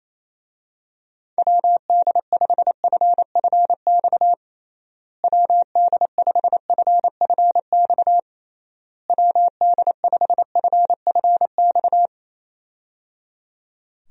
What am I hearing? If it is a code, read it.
Morse code, heard as a keyed tone, about 28 wpm, 700 Hz. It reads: WD5FFX WD5FFX WD5FFX
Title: USA Call Signs - 6 Characters - Repeated 3x - 28wpm